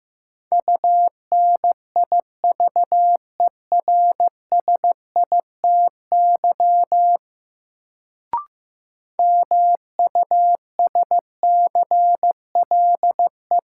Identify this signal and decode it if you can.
Morse code, heard as a keyed tone, about 15 wpm, 700 Hz. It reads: UNIVERSITY MUSCLE
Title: Sentences from Top 1300 Words - New Word Drill Alt - 15wpm